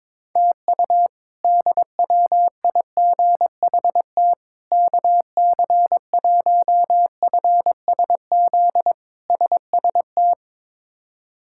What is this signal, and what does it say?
Morse code, heard as a keyed tone, about 22 wpm, 700 Hz. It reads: TU DWIGHT KC1FSZ SST
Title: K1USN SST QSO Practice - 22wpm